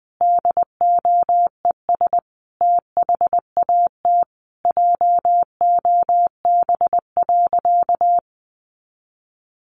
Morse code, heard as a keyed tone, about 20 wpm, 700 Hz. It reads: DOES THAT JOB.
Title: Sentences from Top 700 Words 20wpm